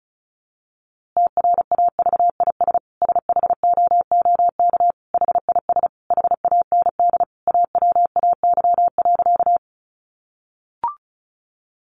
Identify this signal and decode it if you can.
Morse code, heard as a keyed tone, about 34 wpm, 700 Hz. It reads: TRAVIS SHOOK HIS HAND AWAY.
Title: A Sound Of Thunder 35wpm